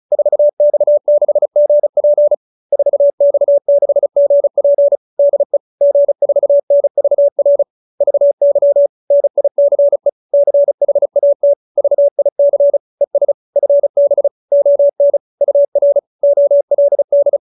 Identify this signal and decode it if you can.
Morse code, heard as a keyed tone, about 34 wpm, 570 Hz. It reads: 4X6GP 4X6GP DE G4NVR VY NICE CHAT VIC ES FB ON UR OLD